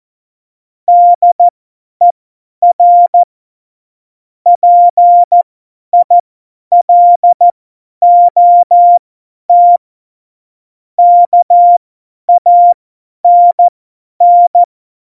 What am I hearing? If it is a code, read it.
Morse code, heard as a keyed tone, about 14 wpm, spaced as for 10 wpm, 700 Hz. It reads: DER PILOT KANN
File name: Morsekurs / Witze / Witz 22. Dezember 2020, 50 F70